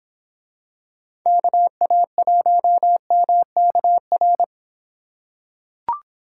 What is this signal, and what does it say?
Morse code, heard as a keyed tone, about 26 wpm, 700 Hz. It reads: KA1MKR